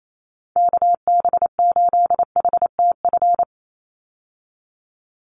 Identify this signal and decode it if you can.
Morse code, heard as a keyed tone, about 28 wpm, 700 Hz. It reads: KB8HTF
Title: USA Call Signs 28wpm